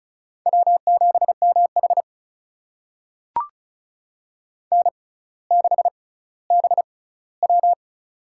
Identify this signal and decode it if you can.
Morse code, heard as a keyed tone, about 36 wpm, spaced as for 18 wpm, 700 Hz. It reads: W7MH N6BW
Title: USA Call Signs - 6x Character Spacing - 35wpm